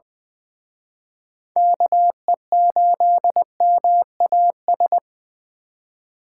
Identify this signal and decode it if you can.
Morse code, heard as a keyed tone, about 24 wpm, 700 Hz. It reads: KE8MAS
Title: Speed Racing - USA 2x3 Call Signs 20wpm